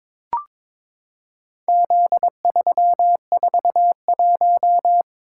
Morse code, heard as a keyed tone, about 23 wpm, 700 Hz. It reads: Z341
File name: practice / 22wpm / Four Letter-Number Combinations - 22wpm